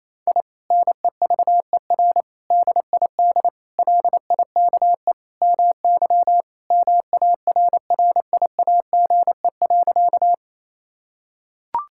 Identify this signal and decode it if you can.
Morse code, heard as a keyed tone, about 28 wpm, 700 Hz. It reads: I NEVER DID LIKE MY MARRIAGE.
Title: Sentences from Top 1000 Words - Review Part 5 - 28wpm